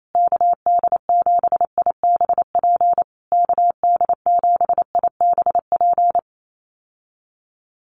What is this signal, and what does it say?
Morse code, heard as a keyed tone, about 28 wpm, 700 Hz. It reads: KD7IBP KD7IBP
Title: USA Call Signs Repeated 3x 28wpm